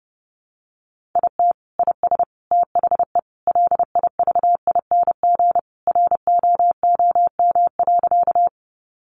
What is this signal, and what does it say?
Morse code, heard as a keyed tone, about 30 wpm, 700 Hz. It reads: IT IS THE LIVING ROOM.
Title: Sentences from Top 1100 Words 30wpm